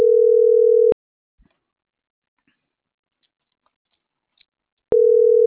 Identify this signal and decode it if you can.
Ringback tone (North America), 440 + 480 Hz.